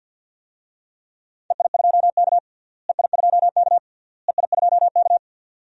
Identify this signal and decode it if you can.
Morse code, heard as a keyed tone, about 50 wpm, 700 Hz. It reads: EI2K EI2K EI2K
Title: DXCC-100 Call Signs - 4 Characters - Repeated 3x - 50wpm